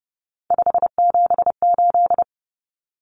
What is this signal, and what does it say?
Morse code, heard as a keyed tone, about 30 wpm, 700 Hz. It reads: 578
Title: Numbers - 3-Digit 30wpm